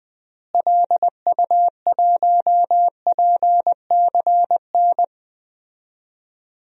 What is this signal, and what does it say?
Morse code, heard as a keyed tone, about 20 wpm, 700 Hz. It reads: LU1PCN
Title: DXCC-100 Call Signs - 6 Characters - Repeated 3x - 20wpm